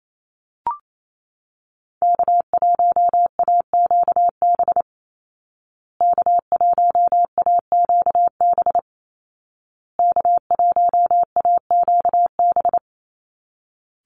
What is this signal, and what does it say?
Morse code, heard as a keyed tone, about 28 wpm, 700 Hz. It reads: K1AQB K1AQB K1AQB
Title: Call Signs Repeated 2x - 28wpm